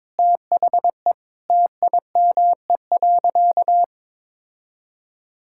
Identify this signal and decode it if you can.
Morse code, heard as a keyed tone, about 21 wpm, 700 Hz. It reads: THE TIME.
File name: Sentences with 9-Word Phrases - 22wpm